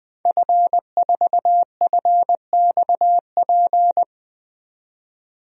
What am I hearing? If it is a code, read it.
Morse code, heard as a keyed tone, about 20 wpm, 700 Hz. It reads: F4FXP
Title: DXCC-100 Call Signs - 5 Characters - 20wpm